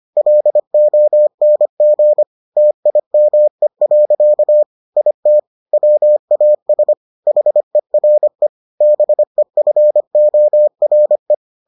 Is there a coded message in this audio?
Yes: Morse code from 0.2 to 11.3 s, heard as a keyed tone, about 25 wpm, 600 Hz. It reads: LONG TIME. IT WAS HERE BEFORE